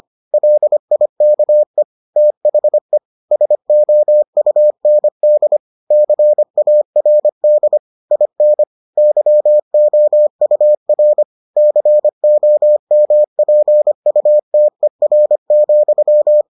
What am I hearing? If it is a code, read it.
Morse code, heard as a keyed tone, about 25 wpm, 600 Hz. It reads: LIKE THE SOUND CARD IN YOUR COMPUTER,